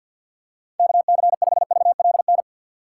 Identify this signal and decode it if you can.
Morse code, heard as a keyed tone, about 50 wpm, 700 Hz. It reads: KC5VLN
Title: USA Call Signs - 6 Characters - 4x Character Spacing - 50wpm